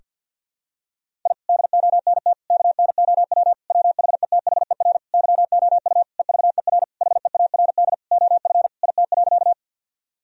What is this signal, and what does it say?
Morse code, heard as a keyed tone, about 52 wpm, 700 Hz. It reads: I DONT KNOW WHETHER YOU EVER HEARD OF IT.
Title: Sentences from Top 1000 Words - Review Part 3 - 50wpm